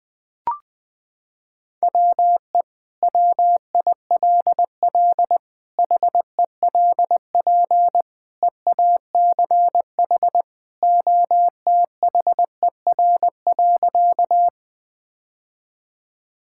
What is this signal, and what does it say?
Morse code, heard as a keyed tone, about 19 wpm, 700 Hz. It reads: WE WILL HELP EACH OTHER.
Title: Sentences with 2-Word Phrases - 20wpm